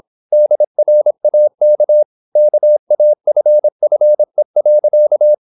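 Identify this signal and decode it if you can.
Morse code, heard as a keyed tone, about 26 wpm, 600 Hz. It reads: DRAK KAFFE.
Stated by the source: High-speed (QRQ) Morse at 26 wpm.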